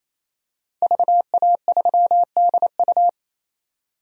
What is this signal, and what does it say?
Morse code, heard as a keyed tone, about 28 wpm, 700 Hz. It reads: VA3DU
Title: DXCC-100 Call Signs - 5 Characters - 28wpm